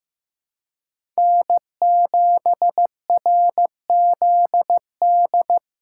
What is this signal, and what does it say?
Morse code, heard as a keyed tone, about 15 wpm, 700 Hz. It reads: N7RZD
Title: USA Call Signs - 5 Characters - 2x Character Spacing - 15wpm